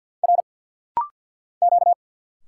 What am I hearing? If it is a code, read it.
Morse code, heard as a keyed tone, about 50 wpm, 700 Hz. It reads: R Q
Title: Single Letter-Number - Rapid-Fire 50wpm